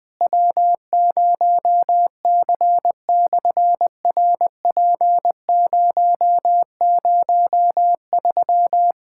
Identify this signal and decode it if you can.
Morse code, heard as a keyed tone, about 18 wpm, 700 Hz. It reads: W0C/RP003
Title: SOTA QSO Practice 20wpm